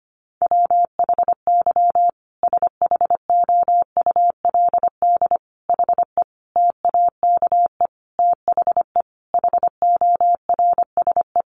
Morse code, heard as a keyed tone, about 25 wpm, 700 Hz. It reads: WHY SHOULD HE TAKE THE HORSE
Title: Sentences from Top 1500 Words 25wpm